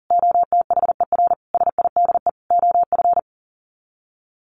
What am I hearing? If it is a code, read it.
Morse code, heard as a keyed tone, about 40 wpm, 700 Hz. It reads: OTHER SIDE OF
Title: Sets of 4 Words 40wpm